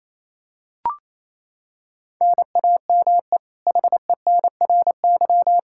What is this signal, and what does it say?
Morse code, heard as a keyed tone, about 28 wpm, 700 Hz. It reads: NAME HENRY
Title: Practice QSO - Exchange 1 - Beginning - 28wpm